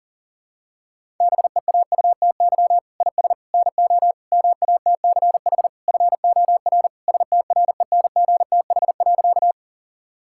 Morse code, heard as a keyed tone, about 37 wpm, 700 Hz. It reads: BEAUTY IS NO MATCH FOR STRENGTH.
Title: Sentences from Top 1600 Words 40wpm